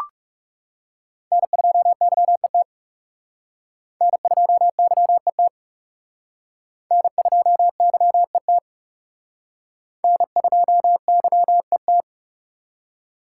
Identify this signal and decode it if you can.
Morse code, heard as a keyed tone, about 36 wpm, 700 Hz. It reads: N2YET N2YET N2YET N2YET
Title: Speed Racing - USA Call Signs 30wpm